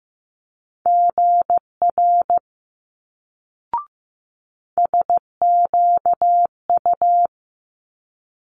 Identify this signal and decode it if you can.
Morse code, heard as a keyed tone, about 15 wpm, 700 Hz. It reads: GR SQU
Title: Multi-letter Phonemes Quick Fire 15wpm